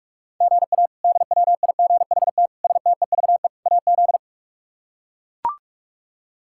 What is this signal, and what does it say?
Morse code, heard as a keyed tone, about 42 wpm, 700 Hz. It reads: GA DWIGHT STEVE AZ
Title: K1USN SST QSO Practice - 45wpm